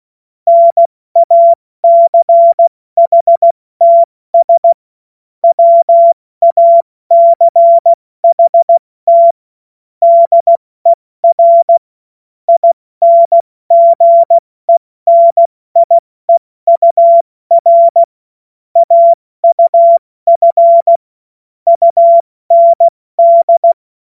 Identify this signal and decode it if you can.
Morse code, heard as a keyed tone, about 16 wpm, spaced as for 14 wpm, 680 Hz. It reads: NACHTS WACHT DER INGENIEUR AUF UND
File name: Morsekurs / Witze / Witz 14. Juni 2022 70 F80